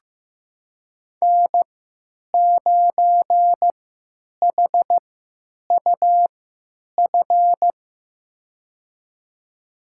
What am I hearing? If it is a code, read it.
Morse code, heard as a keyed tone, about 15 wpm, spaced as for 9 wpm, 700 Hz. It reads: N9HUF
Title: USA Call Signs - 5 Characters - 3x Character Spacing - 15wpm